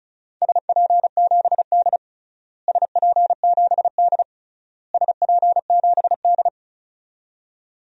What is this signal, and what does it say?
Morse code, heard as a keyed tone, about 34 wpm, 700 Hz. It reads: SP7D SP7D SP7D
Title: DXCC-100 Call Signs - 4 Characters - Repeated 3x - 35wpm